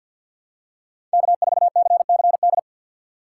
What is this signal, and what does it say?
Morse code, heard as a keyed tone, about 51 wpm, 700 Hz. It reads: K4CCD
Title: USA Call Signs - 5 Characters - 4x Character Spacing - 50wpm